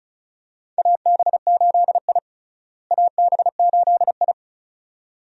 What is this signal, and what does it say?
Morse code, heard as a keyed tone, about 36 wpm, 700 Hz. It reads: AB8I AB8I
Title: USA Call Signs - 4 Characters - Repeated 2x - 35wpm